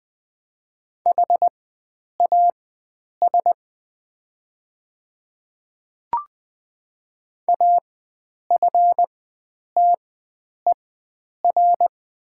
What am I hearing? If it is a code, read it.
Morse code, heard as a keyed tone, about 20 wpm, spaced as for 9 wpm, 700 Hz. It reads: HAS AFTER
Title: 4x Character Spacing - Top 300 Words - 20wpm